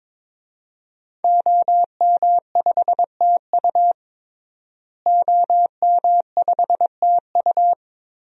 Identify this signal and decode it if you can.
Morse code, heard as a keyed tone, about 22 wpm, 700 Hz. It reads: OM5TU OM5TU
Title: DXCC-100 Call Signs - 5 Characters - Repeated 3x - 22wpm